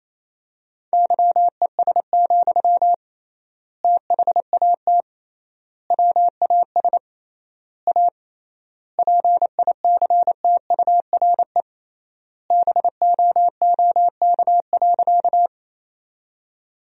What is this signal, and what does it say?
Morse code, heard as a keyed tone, about 28 wpm, 700 Hz. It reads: YES, THAT WAS A PICTURE BOOK.
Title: 3x Word-Spacing - Sentences from Top 500 Words - 28wpm